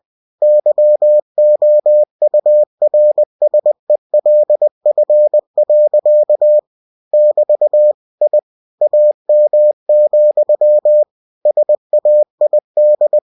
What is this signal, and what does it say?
Morse code, heard as a keyed tone, about 20 wpm, 600 Hz. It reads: YOURSELF. <BT> I AM, SAID
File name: cw-books / winnie-the-pooh / winnie-20wpm-page-0046